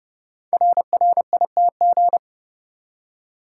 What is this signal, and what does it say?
Morse code, heard as a keyed tone, about 28 wpm, 700 Hz. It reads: RRITG
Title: Code Groups - 5 Letter:Numbers - 30wpm